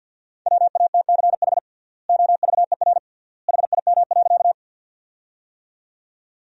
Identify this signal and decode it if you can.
Morse code, heard as a keyed tone, about 52 wpm, 700 Hz. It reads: WATCH OVER HIM.
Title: 3x Word-Spacing - Sentences from Top 300 Words - 50wpm